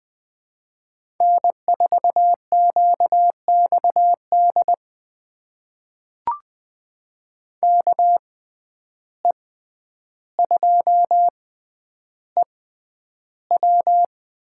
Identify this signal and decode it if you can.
Morse code, heard as a keyed tone, about 20 wpm, spaced as for 10 wpm, 700 Hz. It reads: N4QXD KE2EW
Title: USA Call Signs - 5 Characters - 6x Character Spacing - 20wpm